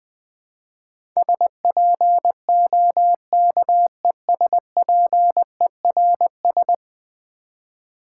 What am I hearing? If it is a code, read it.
Morse code, heard as a keyed tone, about 20 wpm, 700 Hz. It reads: SPOKESPERS
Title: Build A Word - 5 Letters At A Time - 12-Letter Words - 20wpm